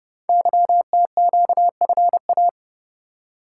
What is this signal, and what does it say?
Morse code, heard as a keyed tone, about 30 wpm, 690 Hz. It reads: YTQFA
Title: Code Groups - 5 Letters - 30wpm